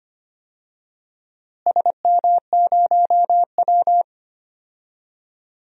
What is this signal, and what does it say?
Morse code, heard as a keyed tone, about 25 wpm, 700 Hz. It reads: SM0W
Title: DXCC-100 Call Signs - 4 Characters - Repeated 3x - 25wpm